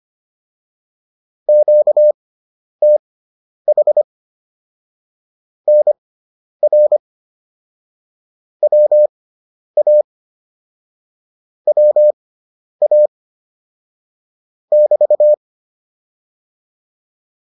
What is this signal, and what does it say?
Morse code, heard as a keyed tone, about 25 wpm, spaced as for 10 wpm, 600 Hz. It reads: QTH NR WA WA <BT>